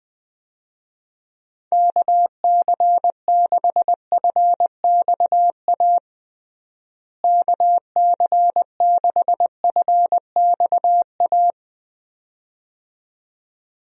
Morse code, heard as a keyed tone, about 20 wpm, 700 Hz. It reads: KC6FXA KC6FXA
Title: USA Call Signs - 6 Characters - Repeated 2x - 20wpm